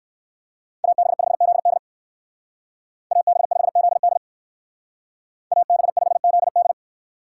Morse code, heard as a keyed tone, about 58 wpm, 700 Hz. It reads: AB5ZD AB5ZD AB5ZD